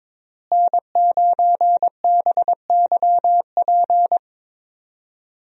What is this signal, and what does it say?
Morse code, heard as a keyed tone, about 22 wpm, 700 Hz. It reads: N9BYP
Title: USA Call Signs - 5 Characters - Repeated 2x - 22wpm